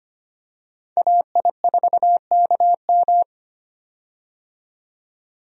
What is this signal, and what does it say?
Morse code, heard as a keyed tone, about 25 wpm, 700 Hz. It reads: AI4KM
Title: USA Call Signs - 5 Characters - 25wpm